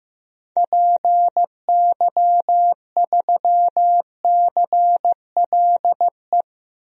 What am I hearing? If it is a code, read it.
Morse code, heard as a keyed tone, about 16 wpm, 700 Hz. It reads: PY3CLE
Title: DXCC-100 Call Signs - 6 Characters - Repeated 2x - 15wpm